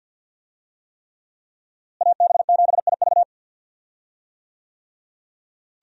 Morse code, heard as a keyed tone, about 52 wpm, 700 Hz. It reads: AB7IV
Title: USA Call Signs - 5 Characters - Repeated 3x - 50wpm